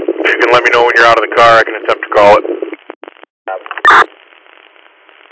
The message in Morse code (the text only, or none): none